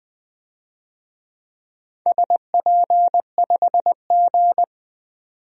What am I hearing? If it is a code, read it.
Morse code, heard as a keyed tone, about 20 wpm, 700 Hz. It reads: SP5G
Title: DXCC-100 Call Signs - 4 Characters - 20wpm